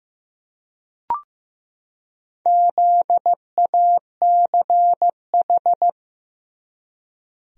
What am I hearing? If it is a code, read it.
Morse code, heard as a keyed tone, about 15 wpm, 700 Hz. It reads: ZACH
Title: USA Common Names - Max 4 Letters - 15wpm